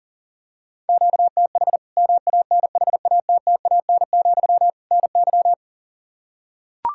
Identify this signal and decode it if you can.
Morse code, heard as a keyed tone, about 40 wpm, 700 Hz. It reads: QTH MANHATTAN, NY
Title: Practice QSO - Exchange 1 - Beginning - 40wpm